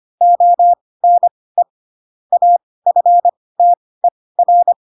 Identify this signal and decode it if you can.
Morse code, heard as a keyed tone, about 25 wpm, spaced as for 18 wpm, 700 Hz. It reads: ONE AFTER
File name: Pinocchio / Pinocchio ch7 12 25 ik1hge